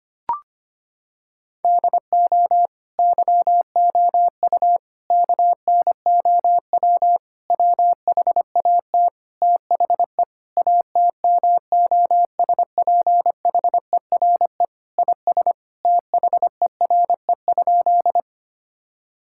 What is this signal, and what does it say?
Morse code, heard as a keyed tone, about 25 wpm, 700 Hz. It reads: DO YOU KNOW WHAT THE ATMOSPHERE IS THERE?
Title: Sentences from Top 1600 Words 25wpm